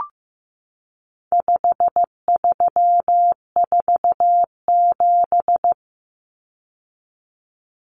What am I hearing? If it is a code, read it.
Morse code, heard as a keyed tone, about 15 wpm, 700 Hz. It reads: 5347
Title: Common Cut Numbers - 4-Digit 15wpm